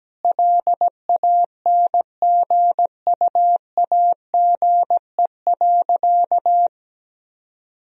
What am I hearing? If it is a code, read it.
Morse code, heard as a keyed tone, about 17 wpm, 700 Hz. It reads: LANGUAGE.
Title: Sentences from Top 1000 Words - Review Part 3 - 17wpm